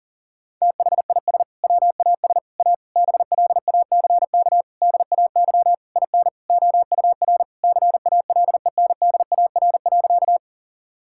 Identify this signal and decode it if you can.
Morse code, heard as a keyed tone, about 37 wpm, 700 Hz. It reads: THIS WAS A BLACK DAY IN OUR CALENDAR.
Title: Sentences from Top 1600 Words 40wpm